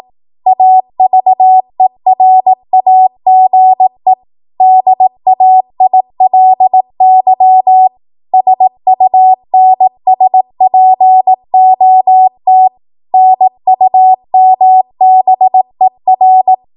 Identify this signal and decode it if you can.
Morse code, heard as a keyed tone, about 18 wpm, 750 Hz. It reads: AVERAGE DAILY SUNSPOT NUMBER
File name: Archive / 18 WPM / 18WPM